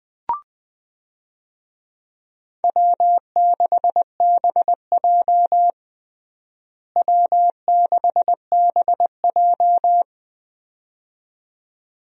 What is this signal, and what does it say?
Morse code, heard as a keyed tone, about 20 wpm, 700 Hz. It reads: W6BJ W6BJ
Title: USA Call Signs - 4 Characters - Repeated 2x - 20wpm